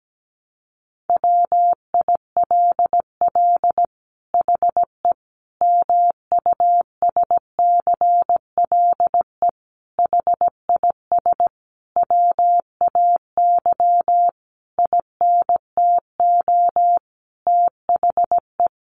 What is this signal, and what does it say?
Morse code, heard as a keyed tone, about 17 wpm, 700 Hz. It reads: WILL HE MUSCLE HIS WAY INTO THE
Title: Sentences from Top 1300 Words 17wpm